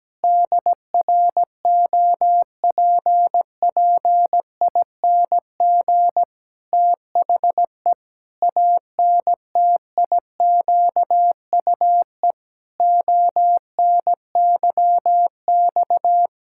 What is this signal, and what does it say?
Morse code, heard as a keyed tone, about 17 wpm, 700 Hz. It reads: DROPPING THE ANTIQUE ONYX
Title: Pangram Challenge v2 - 17wpm